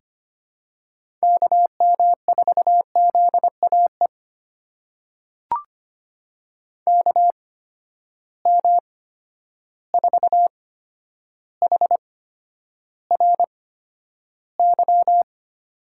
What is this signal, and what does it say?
Morse code, heard as a keyed tone, about 25 wpm, spaced as for 11 wpm, 700 Hz. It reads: KM4ZAE KM4HRY
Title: USA Call Signs - 6 Characters - 8x Character Spacing - 25wpm